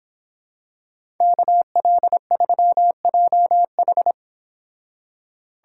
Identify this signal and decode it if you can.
Morse code, heard as a keyed tone, about 26 wpm, 700 Hz. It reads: KL3JH